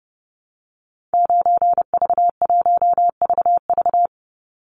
Morse code, heard as a keyed tone, about 30 wpm, 700 Hz. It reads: 9V1VV
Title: Call Signs Repeated 2x - 30wpm